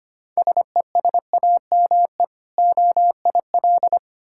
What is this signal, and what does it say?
Morse code, heard as a keyed tone, about 25 wpm, 700 Hz. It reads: SESAME OIL
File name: practice / 25wpm / Difficult Words with Hints - 25wpm